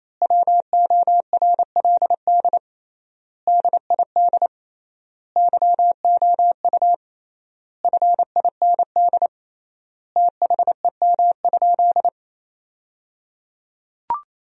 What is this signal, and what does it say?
Morse code, heard as a keyed tone, about 28 wpm, 700 Hz. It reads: WORLD DID YOU FIND THEM?
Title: 3x Word-Spacing - Sentences from Top 200 Words - 28wpm